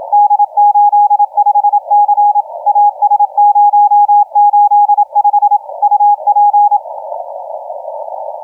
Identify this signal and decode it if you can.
Morse code, heard as a keyed tone, about 27 wpm, 820 Hz, with radio noise. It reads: D85C AS085 UP
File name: D85C 0845UTC 15m